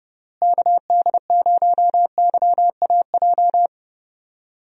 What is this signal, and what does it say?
Morse code, heard as a keyed tone, about 32 wpm, 700 Hz. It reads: KD0YAJ
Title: USA Call Signs - 6 Characters - Repeated 3x - 30wpm